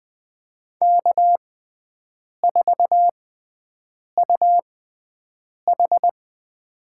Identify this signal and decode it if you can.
Morse code, heard as a keyed tone, about 20 wpm, spaced as for 7 wpm, 700 Hz. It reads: K4UH
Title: USA Call Signs - 4 Characters - 6x Character Spacing - 20wpm